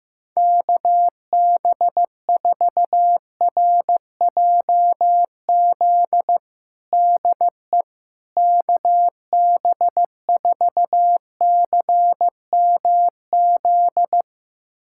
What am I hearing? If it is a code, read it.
Morse code, heard as a keyed tone, about 15 wpm, 700 Hz. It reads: KB4RJZ DE KB4CMZ
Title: USA Callsign DE Callsign - 6 Characters - 15wpm